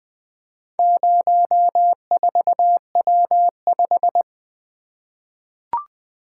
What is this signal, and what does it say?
Morse code, heard as a keyed tone, about 20 wpm, 700 Hz. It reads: 04W5
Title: Four Letter-Number Combinations - 20wpm